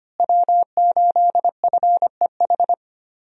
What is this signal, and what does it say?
Morse code, heard as a keyed tone, about 25 wpm, 700 Hz. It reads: W8FEH
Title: USA Call Signs - 5 Characters - Repeated 2x - 25wpm